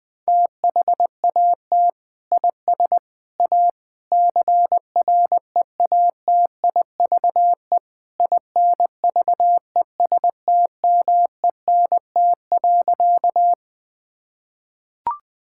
Morse code, heard as a keyed tone, about 20 wpm, 700 Hz. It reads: THAT IS A CREATIVE INVESTMENT.
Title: Sentences from Top 1000 Words - Review Part 5 - 20wpm